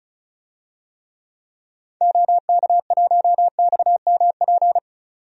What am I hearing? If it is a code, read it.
Morse code, heard as a keyed tone, about 34 wpm, 700 Hz. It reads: OK1XMP
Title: DXCC-100 Call Signs - 6 Characters - 35wpm